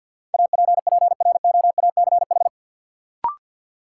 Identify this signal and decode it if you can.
Morse code, heard as a keyed tone, about 53 wpm, 700 Hz. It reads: APPROACH
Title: Sentences from Top 1000 Words - Review Part 6 - New Word Drill - 50wpm